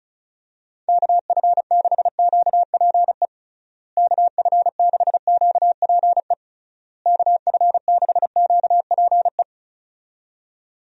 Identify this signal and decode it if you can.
Morse code, heard as a keyed tone, about 34 wpm, 700 Hz. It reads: KF6QPE KF6QPE KF6QPE
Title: USA Call Signs - 6 Characters - Repeated 3x - 35wpm